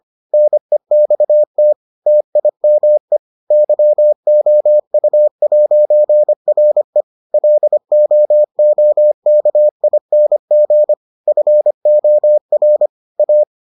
Morse code, heard as a keyed tone, about 25 wpm, 600 Hz. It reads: NEXT TIME YOU'RE LOOKING FOR A